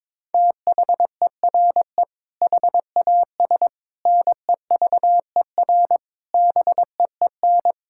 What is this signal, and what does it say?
Morse code, heard as a keyed tone, about 22 wpm, 700 Hz. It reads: THERE HAS NEVER BEEN
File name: practice / 22wpm / Sentences with 8-Word Phrases - 22wpm